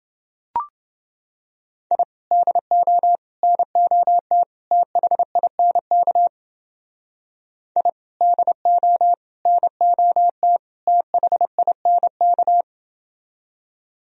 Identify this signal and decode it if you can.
Morse code, heard as a keyed tone, about 29 wpm, 700 Hz. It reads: I DO NOT THINK I DO NOT THINK